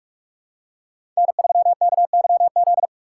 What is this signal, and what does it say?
Morse code, heard as a keyed tone, about 46 wpm, 700 Hz. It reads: N3KYZ
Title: USA Call Signs - 5 Characters - 3x Character Spacing - 45wpm